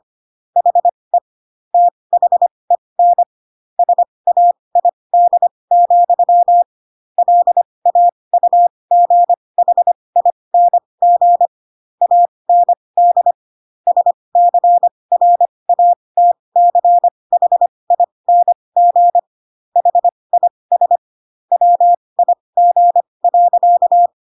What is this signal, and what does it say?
Morse code, heard as a keyed tone, about 25 wpm, spaced as for 20 wpm, 700 Hz. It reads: HE THEN SAID, LAUGHING AND SCRATCHING HIS WIG.
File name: Pinocchio / Pinocchio ch1 15 25 ik1hge